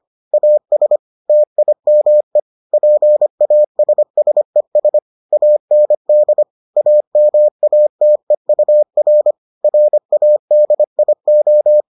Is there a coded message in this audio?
Yes: Morse code from 0.3 to 11.8 s, heard as a keyed tone, about 25 wpm, 600 Hz. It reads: AS TIME PASSES AND AMATEUR RADIO